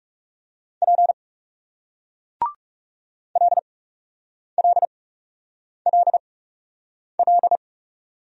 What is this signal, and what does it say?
Morse code, heard as a keyed tone, about 40 wpm, spaced as for 8 wpm, 700 Hz. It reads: P LLLL